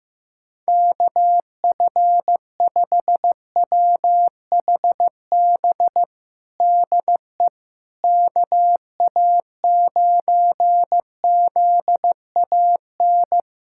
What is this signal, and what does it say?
Morse code, heard as a keyed tone, about 15 wpm, 700 Hz. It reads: KF5WHB DE KA9ZAN
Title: USA Callsign DE Callsign - 6 Characters - 15wpm